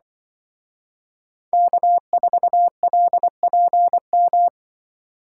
Morse code, heard as a keyed tone, about 24 wpm, 700 Hz. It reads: K4LPM
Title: Speed Racing - USA 1x3 Call Signs 20wpm